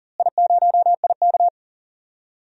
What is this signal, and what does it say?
Morse code, heard as a keyed tone, about 39 wpm, 700 Hz. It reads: I0IK
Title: DXCC-100 Call Signs - 4 Characters - Repeated 3x - 40wpm